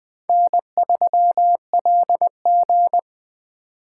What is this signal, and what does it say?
Morse code, heard as a keyed tone, about 20 wpm, 700 Hz. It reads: N3LG
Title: Speed Racing - USA 1x2 Call Signs 17wpm